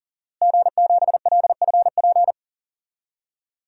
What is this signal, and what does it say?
Morse code, heard as a keyed tone, about 40 wpm, 700 Hz. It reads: G7LFP
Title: DXCC-100 Call Signs - 5 Characters - Repeated 3x - 40wpm